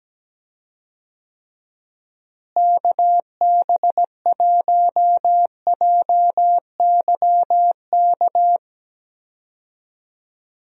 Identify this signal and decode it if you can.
Morse code, heard as a keyed tone, about 17 wpm, 700 Hz. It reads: KB1JYK
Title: USA Call Signs - 6 Characters - Repeated 2x - 17wpm